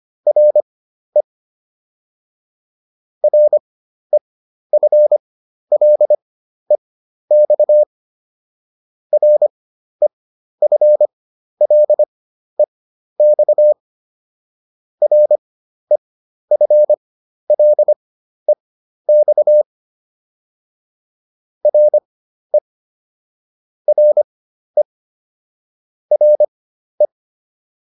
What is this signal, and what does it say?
Morse code, heard as a keyed tone, about 25 wpm, spaced as for 12 wpm, 600 Hz. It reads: RE REFLEX REFLEX REFLEX RE RE RE